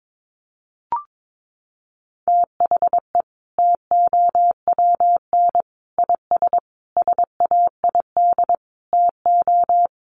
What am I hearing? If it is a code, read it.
Morse code, heard as a keyed tone, about 22 wpm, 700 Hz. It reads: THE TOWN IS SAID TO
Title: Sentences from Top 1300 Words 22wpm